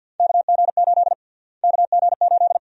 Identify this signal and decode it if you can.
Morse code, heard as a keyed tone, about 52 wpm, 700 Hz. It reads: KG8 KG8
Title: USA Call Sign Prefixes - Repeated 2x - 50wpm